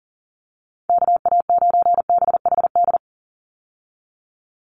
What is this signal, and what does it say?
Morse code, heard as a keyed tone, about 40 wpm, 700 Hz. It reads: KA9BHD
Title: Call Signs - 40wpm